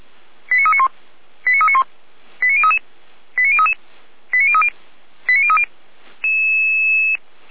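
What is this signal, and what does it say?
Five-tone selective call (ZVEI), heard as quick runs of five tones, tones of about 70 ms. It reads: 89391 89391 89033 89033 89030 89030
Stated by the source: Received by SDR.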